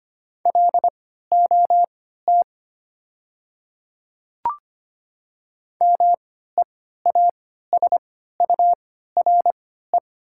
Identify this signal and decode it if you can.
Morse code, heard as a keyed tone, about 25 wpm, spaced as for 14 wpm, 700 Hz. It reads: LOT MEASURE
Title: 3x Character Spacing - Top 500 Words - 25wpm